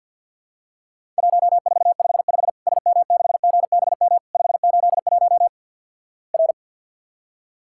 Morse code, heard as a keyed tone, about 50 wpm, 700 Hz. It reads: 1455 SM6GBM 581 R
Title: Worked All Europe DX Contest - QTC Exchange with Repeats - 50wpm